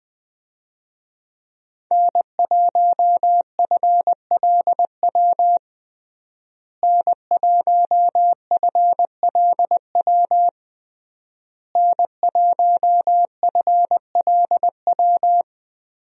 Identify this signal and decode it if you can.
Morse code, heard as a keyed tone, about 20 wpm, 700 Hz. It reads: N1FLW N1FLW N1FLW
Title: USA Call Signs - 5 Characters - Repeated 3x - 20wpm